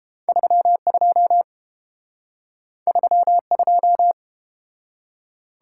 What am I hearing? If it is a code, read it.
Morse code, heard as a keyed tone, about 30 wpm, 700 Hz. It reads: 32 32